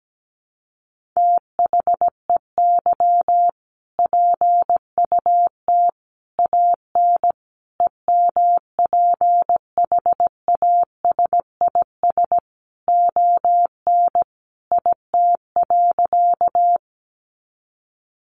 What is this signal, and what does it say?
Morse code, heard as a keyed tone, about 17 wpm, 700 Hz. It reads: THEY PUT AN EMPHASIS ON IT.
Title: Sentences from Top 1400 Words 17wpm